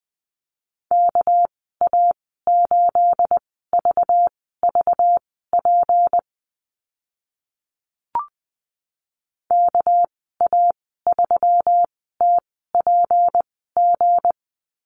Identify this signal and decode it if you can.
Morse code, heard as a keyed tone, about 20 wpm, spaced as for 14 wpm, 700 Hz. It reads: KA8VVP KA3TPG
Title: 213 Lesson 34g - Number 8 - Call Signs 20wpm